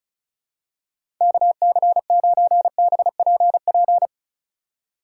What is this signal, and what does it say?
Morse code, heard as a keyed tone, about 36 wpm, 700 Hz. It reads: KC9BPP
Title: USA Call Signs - 6 Characters - 3x Character Spacing - 35wpm